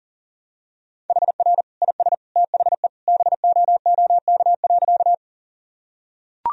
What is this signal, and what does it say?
Morse code, heard as a keyed tone, about 40 wpm, 700 Hz. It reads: HR IS THE BOOK.
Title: Sentences from Top 300 Words and Abbreviations 40wpm